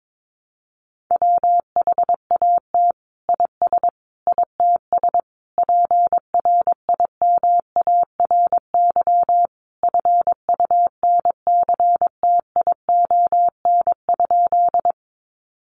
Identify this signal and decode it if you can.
Morse code, heard as a keyed tone, about 22 wpm, 700 Hz. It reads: WHAT IS ITS PRIMARY FUNCTION?
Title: Sentences from Top 1100 Words 22wpm